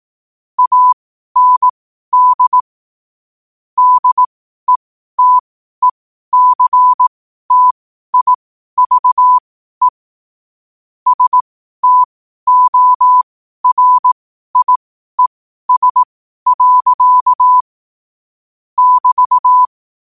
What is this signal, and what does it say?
Morse code, heard as a keyed tone, about 18 wpm, spaced as for 12 wpm, 1000 Hz. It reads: AND DETECTIVE STORIES. <BT>
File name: qotdfiles / 12wpm